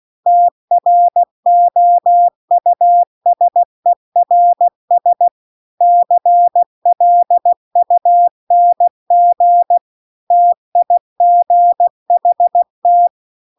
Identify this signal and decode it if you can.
Morse code, heard as a keyed tone, about 16 wpm, 700 Hz. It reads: TROUSERS CLUNG TIGHT
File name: CW-The-Big-Twoheaded-River-2-16WPM